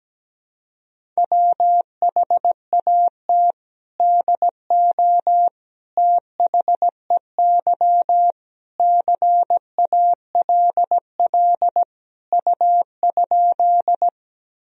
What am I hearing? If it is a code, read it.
Morse code, heard as a keyed tone, about 17 wpm, 700 Hz. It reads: WHAT DO THEY CALL U?
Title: Sentences from Top 100 Words and Abbreviations 17wpm